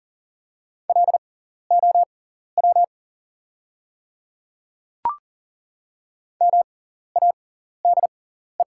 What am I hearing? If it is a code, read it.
Morse code, heard as a keyed tone, about 40 wpm, spaced as for 14 wpm, 690 Hz. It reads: LOW MADE
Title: 6x Character Spacing - Top 200 Words - 40wpm